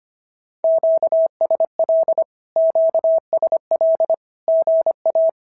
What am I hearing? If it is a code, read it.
Morse code, heard as a keyed tone, about 24 wpm, 650 Hz. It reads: QSL QSL GA
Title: SOTA QSO Practice 25wpm